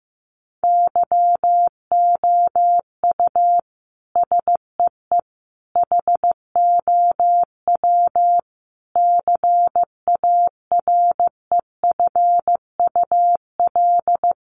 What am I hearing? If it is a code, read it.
Morse code, heard as a keyed tone, about 15 wpm, 700 Hz. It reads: YOU SEE HOW CAREFUL
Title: A Sound Of Thunder 15wpm